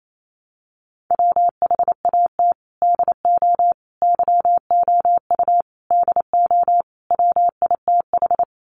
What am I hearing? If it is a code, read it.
Morse code, heard as a keyed tone, about 28 wpm, 700 Hz. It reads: WHAT DO YOU DO WITH
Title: Sets of 5 Words Encore2 28wpm